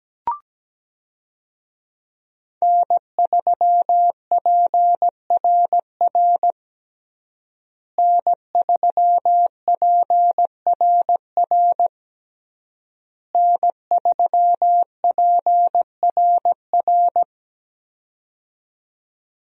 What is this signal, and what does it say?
Morse code, heard as a keyed tone, about 17 wpm, 700 Hz. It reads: N3PRR N3PRR N3PRR
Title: USA Call Signs - 5 Characters - Repeated 3x - 17wpm